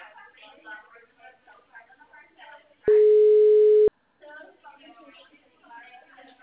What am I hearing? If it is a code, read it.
Ringback tone (Europe, 425 Hz), with speech.
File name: out- -2019- - - .36